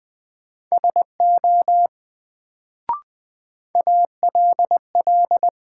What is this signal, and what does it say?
Morse code, heard as a keyed tone, about 20 wpm, 700 Hz. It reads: SO ALL
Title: Sets of 2 Words - New Word Drill - 20wpm